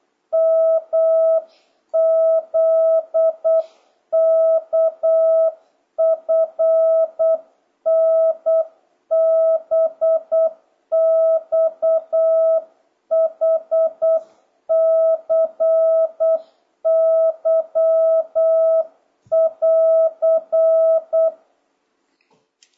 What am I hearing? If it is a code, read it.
Morse code, heard as a keyed tone, about 8 wpm, 650 Hz. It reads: MZKFNBXHCY<AR>
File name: jh8cbh / 2026cw